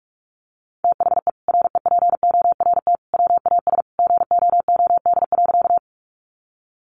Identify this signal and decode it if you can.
Morse code, heard as a keyed tone, about 44 wpm, 700 Hz. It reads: THE REPORT WAS GOOD.
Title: Sentences from Top 900 Words 45wpm